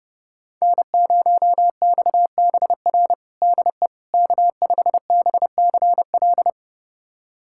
Morse code, heard as a keyed tone, about 30 wpm, 700 Hz. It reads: N0XBR DE K5BCL
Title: USA Callsign DE Callsign - 5 Characters - 30wpm